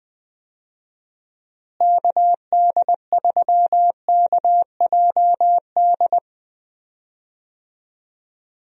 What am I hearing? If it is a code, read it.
Morse code, heard as a keyed tone, about 20 wpm, 700 Hz. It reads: KD3KJD
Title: USA Call Signs - 6 Characters - Repeated 2x - 20wpm